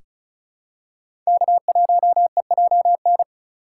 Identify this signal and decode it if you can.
Morse code, heard as a keyed tone, about 34 wpm, 700 Hz. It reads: K1EJN
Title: USA Call Signs - 3x Character Spacing - 35wpm